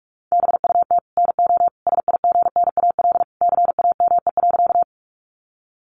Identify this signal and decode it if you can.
Morse code, heard as a keyed tone, about 44 wpm, 700 Hz. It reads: BUT NO SIGNAL CAME.
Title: Sentences from Top 1400 Words 45wpm